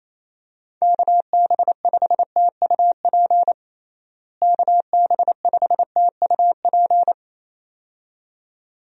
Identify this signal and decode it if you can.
Morse code, heard as a keyed tone, about 27 wpm, 700 Hz. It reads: KB5TUP KB5TUP
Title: USA Call Signs - 6 Characters - Repeated 2x - 28wpm